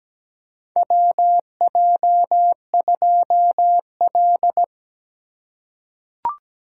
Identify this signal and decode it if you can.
Morse code, heard as a keyed tone, about 17 wpm, 700 Hz. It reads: WJ2L
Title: K1USN SST QSO Practice - 17wpm